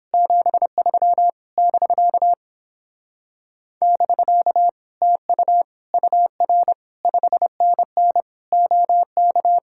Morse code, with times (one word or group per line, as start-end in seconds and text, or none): 0.1-1.3 73
1.6-2.3 <BK>
3.8-4.7 <BK>
5.0-5.6 TU
5.9-6.7 UR
7.1-8.2 5NN
8.5-9.6 OK